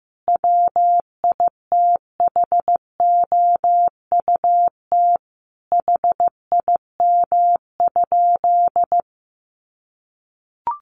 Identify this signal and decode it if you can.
Morse code, heard as a keyed tone, about 15 wpm, 700 Hz. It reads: WITHOUT HIM?
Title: Sentences from Top 1500 Words 15wpm